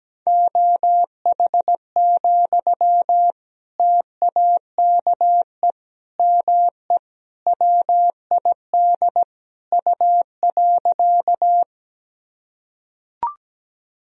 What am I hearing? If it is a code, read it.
Morse code, heard as a keyed tone, about 17 wpm, 700 Hz. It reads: OH, TAKE ME WID U.
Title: Sentences from Top 500 Words and Abbreviations 17wpm